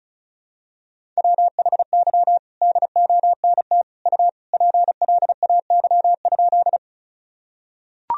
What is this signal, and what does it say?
Morse code, heard as a keyed tone, about 36 wpm, 700 Hz. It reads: WHY DONT U PLAY?
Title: Sentences from Top 300 Words and Abbreviations 35wpm